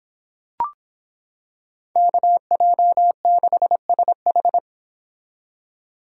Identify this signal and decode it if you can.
Morse code, heard as a keyed tone, about 26 wpm, 700 Hz. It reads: KJ6SH